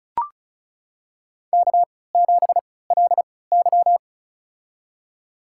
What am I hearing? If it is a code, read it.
Morse code, heard as a keyed tone, about 36 wpm, spaced as for 20 wpm, 700 Hz. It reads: K7LY
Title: USA Call Signs - 4 Characters - 3x Character Spacing - 35wpm